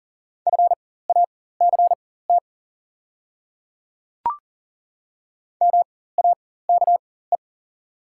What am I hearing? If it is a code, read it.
Morse code, heard as a keyed tone, about 40 wpm, spaced as for 19 wpm, 690 Hz. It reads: FACT MAKE
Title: 4x Character Spacing - Top 500 Words - 40wpm